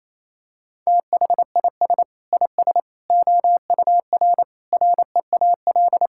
Morse code, heard as a keyed tone, about 28 wpm, 700 Hz. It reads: THIS IS OUR REAL
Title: Sentences from Top 1000 Words - Review Part 5 - 28wpm